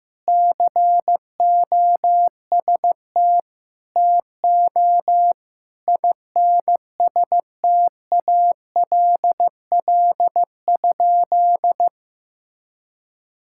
Morse code, heard as a keyed tone, about 15 wpm, 700 Hz. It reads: COST TO INSTALL?
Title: Sentences from Top 1600 Words 15wpm